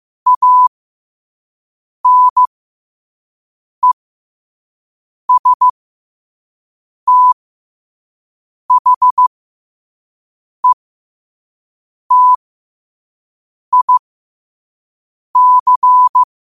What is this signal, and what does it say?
Morse code, heard as a keyed tone, about 15 wpm, spaced as for 5 wpm, 1000 Hz. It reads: ANESTHETIC
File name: 5wpm